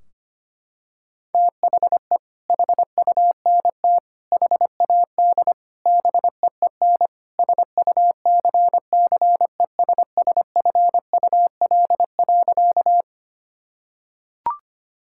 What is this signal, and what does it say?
Morse code, heard as a keyed tone, about 25 wpm, 700 Hz. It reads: THE HUNT HAD BEEN SUCCESSFUL.
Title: Sentences from Top 1600 Words 25wpm